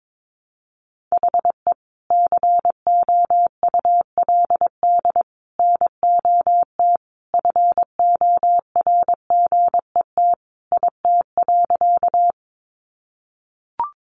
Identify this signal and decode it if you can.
Morse code, heard as a keyed tone, about 22 wpm, 700 Hz. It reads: HE COULD NOT FORGET IT.
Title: Sentences from Top 1200 Words 22wpm